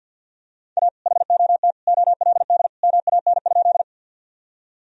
Morse code, heard as a keyed tone, about 50 wpm, 700 Hz. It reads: A HOT OLD MAN?
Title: Sentences from Top 1000 Words - Review Part 3 - 50wpm